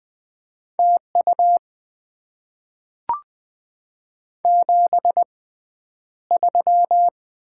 Morse code, heard as a keyed tone, about 20 wpm, spaced as for 10 wpm, 690 Hz. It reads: TU 73
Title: Top 20 QSO Elements - 6x Character Spacing - 20wpm